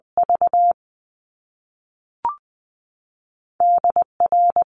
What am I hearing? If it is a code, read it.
Morse code, heard as a keyed tone, about 20 wpm, 700 Hz. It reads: V D R
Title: Top Two Letter Combinations in Words 20wpm